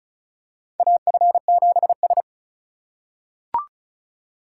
Morse code, heard as a keyed tone, about 34 wpm, 700 Hz. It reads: AF7S
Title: USA Call Signs - 4 Characters - 4x Character Spacing - 35wpm